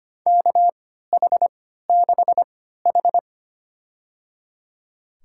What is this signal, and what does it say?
Morse code, heard as a keyed tone, about 25 wpm, spaced as for 14 wpm, 700 Hz. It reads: KH6H
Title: USA Call Signs - 4 Characters - 3x Character Spacing - 25wpm